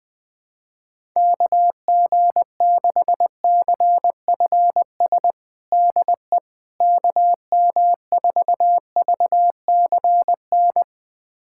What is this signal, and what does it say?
Morse code, heard as a keyed tone, about 21 wpm, 700 Hz. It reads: KG6CFS DE KM4VCN
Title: USA Callsign DE Callsign - 6 Characters - 20wpm